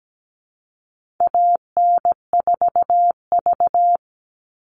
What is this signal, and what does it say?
Morse code, heard as a keyed tone, about 17 wpm, 700 Hz. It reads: AN4V
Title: All Cut Numbers - 4-Digit 17wpm